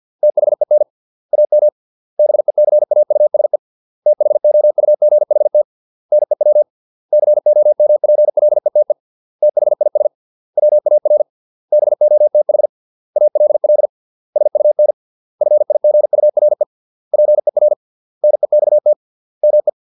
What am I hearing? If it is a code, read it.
Morse code, heard as a keyed tone, about 50 wpm, 600 Hz. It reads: THEN AM BECAUSE THOUGHT NEW COMPLETE THIS WAR BOTH ALL SUN FIGURE PER NEXT ME